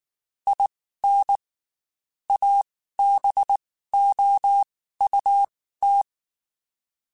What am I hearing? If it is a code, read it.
Morse code, heard as a keyed tone, about 19 wpm, spaced as for 14 wpm, 800 Hz. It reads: IN ABOUT
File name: QSOs / QSOs-mp3s / WD0BGP 1@18